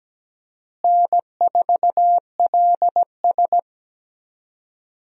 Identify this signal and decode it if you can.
Morse code, heard as a keyed tone, about 17 wpm, 700 Hz. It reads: N4LS
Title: USA Call Signs - 4 Characters - Repeated 3x - 17wpm